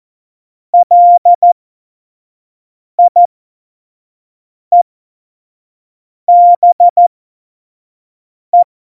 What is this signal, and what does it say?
Morse code, heard as a keyed tone, about 14 wpm, spaced as for 5 wpm, 700 Hz. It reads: LIEBE
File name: Morsekurs / Witze / Witz 14. Januar 2020, 25 F70